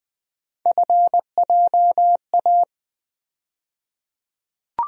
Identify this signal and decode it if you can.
Morse code, heard as a keyed tone, about 20 wpm, 700 Hz. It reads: FJA
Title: USA Call Sign Postfixes - 3 Letters 20wpm